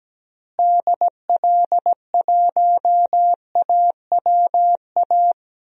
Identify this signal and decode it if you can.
Morse code, heard as a keyed tone, about 17 wpm, 700 Hz. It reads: DL1AWA
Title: DXCC-100 Call Signs - 6 Characters - Repeated 3x - 17wpm